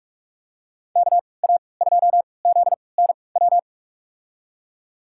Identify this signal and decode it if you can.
Morse code, heard as a keyed tone, about 46 wpm, spaced as for 25 wpm, 690 Hz. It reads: KA2ZNW
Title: USA Call Signs - 6 Characters - 3x Character Spacing - 45wpm